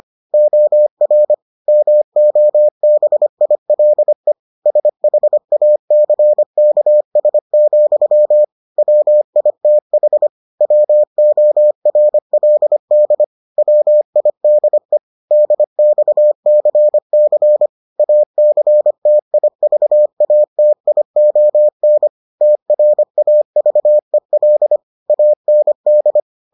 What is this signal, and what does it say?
Morse code, heard as a keyed tone, about 25 wpm, 600 Hz. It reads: OR MOBILE SHACKS, WITH WORLD WIDE DXCC ACTIVATION TRAVEL AND